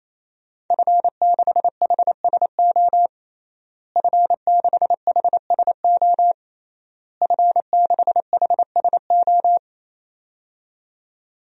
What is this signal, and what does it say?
Morse code, heard as a keyed tone, about 28 wpm, 700 Hz. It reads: F6HSO F6HSO F6HSO
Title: DXCC-100 Call Signs - 5 Characters - Repeated 3x - 28wpm